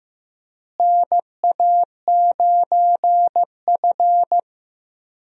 Morse code, heard as a keyed tone, about 15 wpm, 700 Hz. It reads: NA9F
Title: USA Call Signs - 4 Characters - Repeated 2x - 15wpm